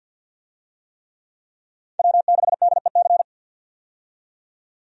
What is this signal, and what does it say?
Morse code, heard as a keyed tone, about 52 wpm, 700 Hz. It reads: W6DEC
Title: USA Call Signs - 5 Characters - 50wpm